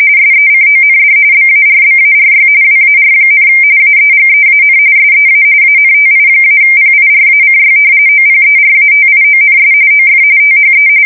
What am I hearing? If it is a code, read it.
RTTY, about 45.45 baud. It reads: SECOND BARBARY WAR, AEGEAN SEA ANTI-PIRACY OPERATIONS OF THE
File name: fsk170 / usnavy2